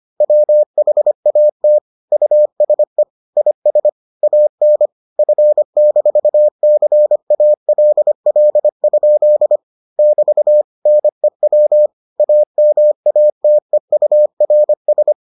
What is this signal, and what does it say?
Morse code, heard as a keyed tone, about 25 wpm, 600 Hz. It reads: WHAT USE IS AN F-CALL? <BT> NEW AMATEURS